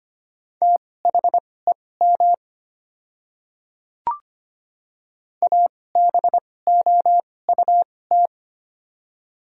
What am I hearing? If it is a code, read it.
Morse code, heard as a keyed tone, about 25 wpm, spaced as for 18 wpm, 700 Hz. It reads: THEM ABOUT
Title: 2x Character Spacing - Top 200 Words - 25wpm